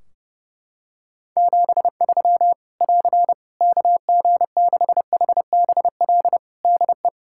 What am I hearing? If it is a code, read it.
Morse code, heard as a keyed tone, about 30 wpm, 700 Hz. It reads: 73 <AR> KG6HBL DE
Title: Practice QSO - Exchange 4 - Ending - 30wpm